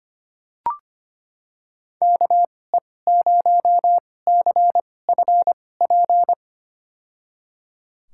Morse code, heard as a keyed tone, about 25 wpm, spaced as for 18 wpm, 700 Hz. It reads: KE0CFP
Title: USA Call Signs - 2x Character Spacing - 25wpm